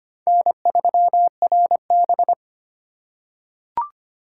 Morse code, heard as a keyed tone, about 25 wpm, 700 Hz. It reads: N3RB
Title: USA Call Signs - 2x Character Spacing - 25wpm